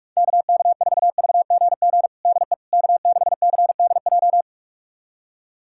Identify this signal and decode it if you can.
Morse code, heard as a keyed tone, about 46 wpm, 700 Hz. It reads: KK4VGG DE K6CDJ
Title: USA Callsign DE Callsign 45wpm